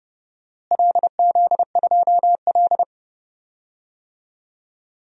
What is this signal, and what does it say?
Morse code, heard as a keyed tone, about 32 wpm, 700 Hz. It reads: LZ2L
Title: DXCC-100 Call Signs - 4 Characters - Repeated 3x - 30wpm